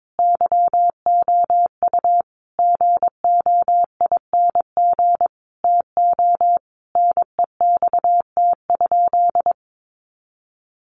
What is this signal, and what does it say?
Morse code, heard as a keyed tone, about 22 wpm, 700 Hz. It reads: YOU GOING TO NEXT?
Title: Sentences from Top 1200 Words 22wpm